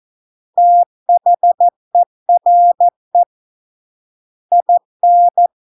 Morse code, heard as a keyed tone, about 14 wpm, 700 Hz. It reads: THERE IN